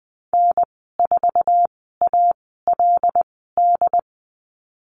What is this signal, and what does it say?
Morse code, heard as a keyed tone, about 20 wpm, spaced as for 14 wpm, 700 Hz. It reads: N4ALD
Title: 076 Lesson 13g - Letter L - Call Signs 20wpm